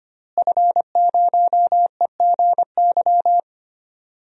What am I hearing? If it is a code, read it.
Morse code, heard as a keyed tone, about 25 wpm, 700 Hz. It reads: F0EGY
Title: DXCC-100 Call Signs - 5 Characters - Repeated 2x - 25wpm